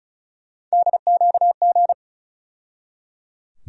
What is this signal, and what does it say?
Morse code, heard as a keyed tone, about 34 wpm, 700 Hz. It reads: DQG
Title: USA Call Sign Postfixes - 3 Letters 35wpm